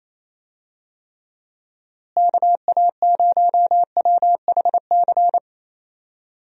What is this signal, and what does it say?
Morse code, heard as a keyed tone, about 27 wpm, 700 Hz. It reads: KA0WHC
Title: USA Call Signs - 6 Characters - Repeated 2x - 28wpm